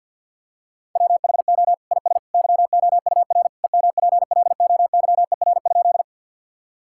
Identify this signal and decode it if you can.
Morse code, heard as a keyed tone, about 50 wpm, 700 Hz. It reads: WHO IS YOUR EMPLOYER?
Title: Sentences from Top 1600 Words 50wpm